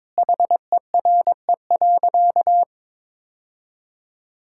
Morse code, heard as a keyed tone, about 22 wpm, 700 Hz. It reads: HERE.
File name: practice / 22wpm / Sentences with 2-Word Phrases - 22wpm